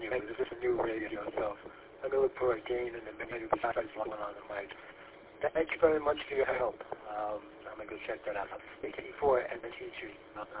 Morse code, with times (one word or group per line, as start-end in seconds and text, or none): none